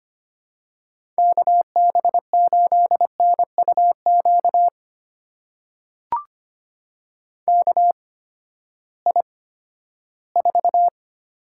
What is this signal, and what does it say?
Morse code, heard as a keyed tone, about 25 wpm, spaced as for 14 wpm, 700 Hz. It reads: KB8NUQ KI4
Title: USA Call Signs - 6 Characters - 8x Character Spacing - 25wpm